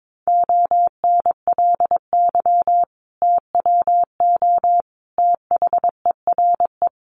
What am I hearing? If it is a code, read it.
Morse code, heard as a keyed tone, about 22 wpm, 700 Hz. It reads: ONLY TWO THERE
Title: Sentences from Top 200 Words 22wpm